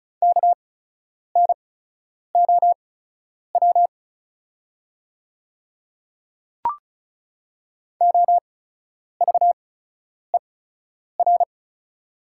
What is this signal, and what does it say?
Morse code, heard as a keyed tone, about 36 wpm, spaced as for 10 wpm, 700 Hz. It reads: KNOW OVER
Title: 8x Character Spacing - Top 100 Words - 35wpm